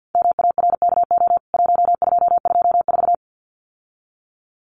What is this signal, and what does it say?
Morse code, heard as a keyed tone, about 50 wpm, 700 Hz. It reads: MARKO 1224
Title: CWT Exchange 50wpm